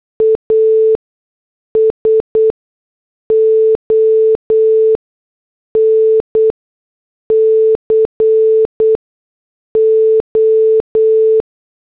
Morse code, heard as a keyed tone, about 8 wpm, spaced as for 6 wpm, 440 Hz. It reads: ASON CO